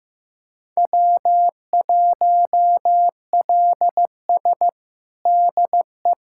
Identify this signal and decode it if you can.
Morse code, heard as a keyed tone, about 15 wpm, 700 Hz. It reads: W1LS DE
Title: USA Callsign DE Callsign - 4 Characters - 15wpm